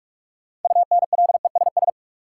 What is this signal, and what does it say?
Morse code, heard as a keyed tone, about 47 wpm, 700 Hz. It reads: UNLESS
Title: Sentences from Top 1000 Words - Review Part 5 - New Word Drill - 45wpm